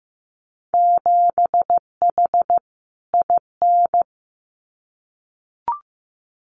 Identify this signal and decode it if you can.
Morse code, heard as a keyed tone, about 15 wpm, 700 Hz. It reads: 7H IN
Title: Winter Field Day Exchanges Encore 15wpm